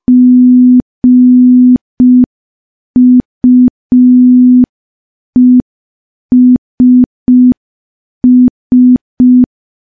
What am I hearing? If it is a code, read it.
Morse code, heard as a keyed tone, about 5 wpm, 250 Hz. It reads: GUESS
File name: morse / morse-training-05WPM